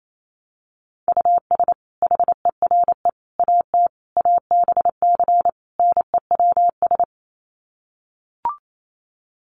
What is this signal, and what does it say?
Morse code, heard as a keyed tone, about 28 wpm, 700 Hz. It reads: US HERE AT ABC NEWS
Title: Sets of 5 Words 28wpm